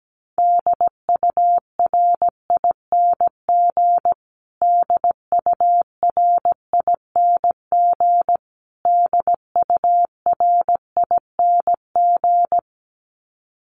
Morse code, heard as a keyed tone, about 17 wpm, 700 Hz. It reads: DURING DURING DURING
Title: Top 500 Words Repeated 17wpm